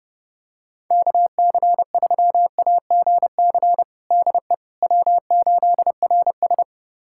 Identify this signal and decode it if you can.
Morse code, heard as a keyed tone, about 30 wpm, 700 Hz. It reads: KC3AGC DE W8RS
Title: USA Callsign DE Callsign 30wpm